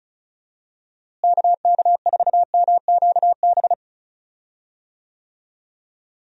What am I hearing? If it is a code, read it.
Morse code, heard as a keyed tone, about 36 wpm, 700 Hz. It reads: KK4MQB
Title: USA Call Signs - 6 Characters - Repeated 2x - 35wpm